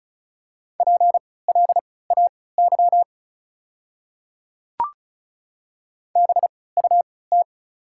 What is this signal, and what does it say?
Morse code, heard as a keyed tone, about 36 wpm, spaced as for 20 wpm, 690 Hz. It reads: PLAY BUT
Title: 3x Character Spacing - Top 500 Words - 35wpm